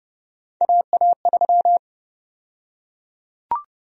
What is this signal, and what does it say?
Morse code, heard as a keyed tone, about 30 wpm, 700 Hz. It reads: AA3
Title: USA Call Sign Prefixes - 30wpm